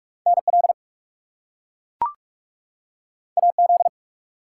Morse code, heard as a keyed tone, about 43 wpm, 700 Hz. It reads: NL AZ
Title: US States and Canadian Provinces 45wpm